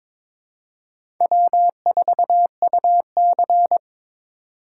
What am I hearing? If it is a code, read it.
Morse code, heard as a keyed tone, about 22 wpm, 700 Hz. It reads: W4UC
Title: USA Call Signs - 4x Character Spacing - 22wpm